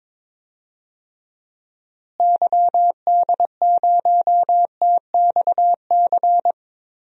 Morse code, heard as a keyed tone, about 22 wpm, 700 Hz. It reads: YD0TXC
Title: DXCC-100 Call Signs - 6 Characters - Repeated 3x - 22wpm